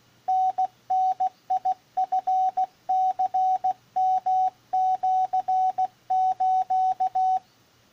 Morse code (Japanese Wabun code), heard as a keyed tone, about 16 wpm, 750 Hz, with radio noise. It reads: タダチニヨシス